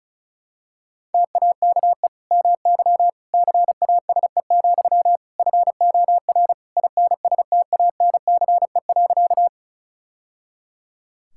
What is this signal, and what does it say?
Morse code, heard as a keyed tone, about 36 wpm, 700 Hz. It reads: TAKE MY CASE, FOR INSTANCE.
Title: Sentences from Top 1000 Words - Review Part 5 - 35wpm